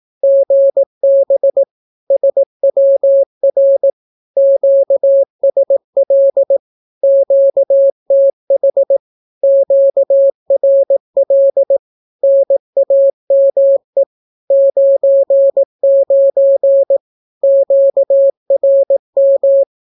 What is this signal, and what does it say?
Morse code, heard as a keyed tone, about 18 wpm, 550 Hz. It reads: GB SWR QSL QTH QRL NAME 99 QRM
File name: naukapodstaw / skroty / skroty